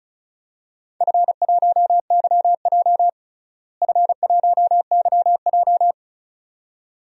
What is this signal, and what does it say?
Morse code, heard as a keyed tone, about 36 wpm, 700 Hz. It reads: F1YJ F1YJ
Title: DXCC-100 Call Signs - 4 Characters - Repeated 2x - 35wpm